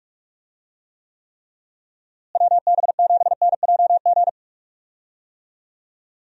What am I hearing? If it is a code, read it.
Morse code, heard as a keyed tone, about 44 wpm, 700 Hz. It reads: WB7NJG
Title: USA Call Signs - 6 Characters - Repeated 3x - 45wpm